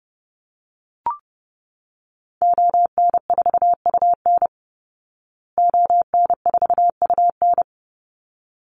Morse code, heard as a keyed tone, about 30 wpm, 700 Hz. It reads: ON4UN ON4UN
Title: Call Signs Repeated 3x - 30wpm